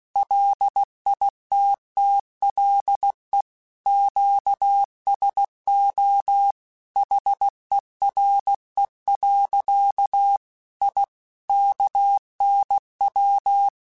Morse code, heard as a keyed tone, about 16 wpm, 800 Hz. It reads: LITTLE QSO HERE. I KNW
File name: QSOs / QSOs-mp3s / WD0BGP 2@15